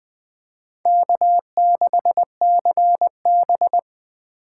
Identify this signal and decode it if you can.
Morse code, heard as a keyed tone, about 20 wpm, 700 Hz. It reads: K6CB